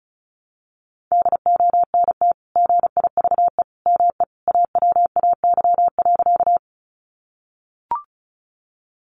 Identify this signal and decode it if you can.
Morse code, heard as a keyed tone, about 34 wpm, 700 Hz. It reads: DONT GIVE ME AWAY.
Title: Sentences from Top 600 Words 35wpm